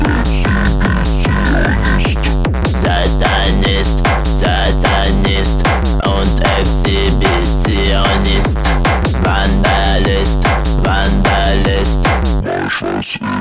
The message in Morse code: none